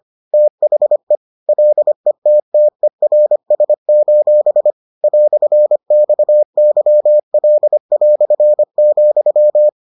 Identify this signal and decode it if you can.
Morse code, heard as a keyed tone, about 25 wpm, 600 Hz. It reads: THE LETTERS: "XYL",